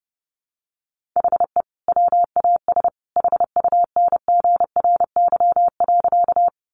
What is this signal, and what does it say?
Morse code, heard as a keyed tone, about 30 wpm, 700 Hz. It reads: HE WAS HUNGRY.
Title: Sentences from Top 1400 Words 30wpm